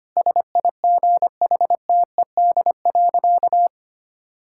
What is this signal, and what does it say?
Morse code, heard as a keyed tone, about 25 wpm, 700 Hz. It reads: SIGHTED.
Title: Pangram Challenge v2 - 25wpm